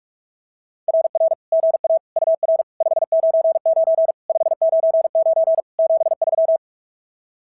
Morse code, heard as a keyed tone, about 46 wpm, 650 Hz. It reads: RR GA UR 599 599 73
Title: SOTA QSO Practice 45wpm